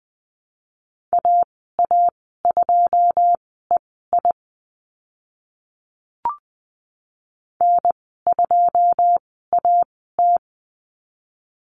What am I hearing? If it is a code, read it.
Morse code, heard as a keyed tone, about 20 wpm, spaced as for 15 wpm, 700 Hz. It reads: AA2EI N2AT
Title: 082 Lesson 14g - Number 2 - Call Signs 20wpm